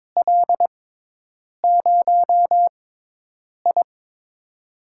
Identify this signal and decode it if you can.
Morse code, heard as a keyed tone, about 22 wpm, spaced as for 8 wpm, 700 Hz. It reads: L0I
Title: USA Call Signs - 4 Characters - 6x Character Spacing - 22wpm